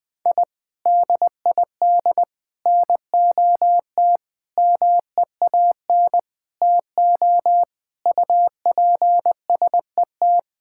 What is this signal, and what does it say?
Morse code, heard as a keyed tone, about 20 wpm, 700 Hz. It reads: I DID NOT MEAN TO UPSET
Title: Sentences with 5-Word Phrases - Spoken Beginning - 20wpm